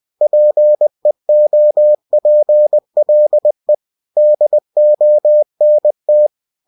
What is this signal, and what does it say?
Morse code, heard as a keyed tone, about 20 wpm, 600 Hz. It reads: PEOPLE DONT
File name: cw-books / winnie-the-pooh / winnie-20wpm-page-0093